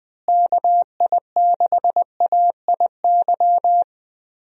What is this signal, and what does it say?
Morse code, heard as a keyed tone, about 20 wpm, 700 Hz. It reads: KI6AIY
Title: USA Call Signs - 6 Characters - Repeated 3x - 20wpm